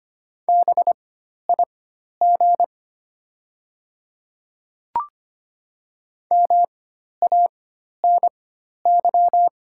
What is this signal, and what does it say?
Morse code, heard as a keyed tone, about 25 wpm, spaced as for 12 wpm, 700 Hz. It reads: BIG MANY
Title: 4x Character Spacing - Top 200 Words - 25wpm